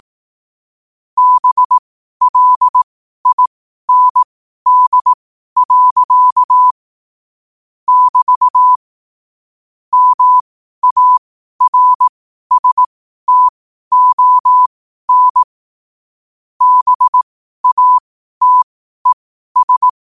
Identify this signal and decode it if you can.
Morse code, heard as a keyed tone, about 18 wpm, spaced as for 12 wpm, 1000 Hz. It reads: BLIND. <BT> MARSTON BATES
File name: qotdfiles / 12wpm